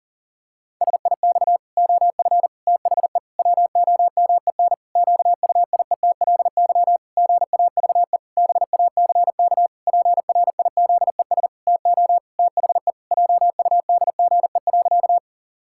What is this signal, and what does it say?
Morse code, heard as a keyed tone, about 40 wpm, 700 Hz. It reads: SIX OF THE WOMEN QUIETLY GAVE BACK PRIZES TO THE JUDGE.
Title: Pangram Challenge v2 - 40wpm